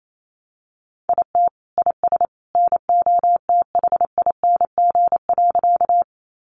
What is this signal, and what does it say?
Morse code, heard as a keyed tone, about 28 wpm, 700 Hz. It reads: IT IS NOTHING.
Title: Sentences from Top 500 Words 28wpm